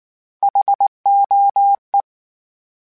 Morse code, heard as a keyed tone, about 19 wpm, 800 Hz. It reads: HOE
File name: H Prefixes / H mp3s